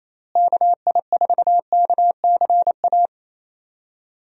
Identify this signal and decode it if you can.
Morse code, heard as a keyed tone, about 28 wpm, 700 Hz. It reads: KI4KCA
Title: USA Call Signs - 6 Characters - Repeated 3x - 28wpm